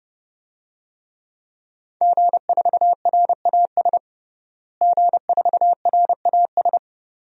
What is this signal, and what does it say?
Morse code, heard as a keyed tone, about 30 wpm, 700 Hz. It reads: G4RAS G4RAS
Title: DXCC-100 Call Signs - 5 Characters - Repeated 2x - 30wpm